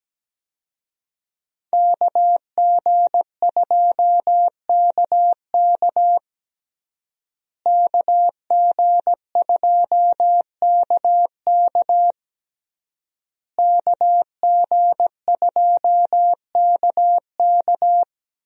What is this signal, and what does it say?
Morse code, heard as a keyed tone, about 17 wpm, 700 Hz. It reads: KG2KK KG2KK KG2KK
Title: USA Call Signs - 5 Characters - Repeated 3x - 17wpm